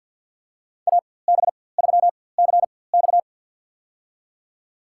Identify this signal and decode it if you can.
Morse code, heard as a keyed tone, about 53 wpm, spaced as for 24 wpm, 700 Hz. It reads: AB3CX
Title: USA Call Signs - 4x Character Spacing - 50wpm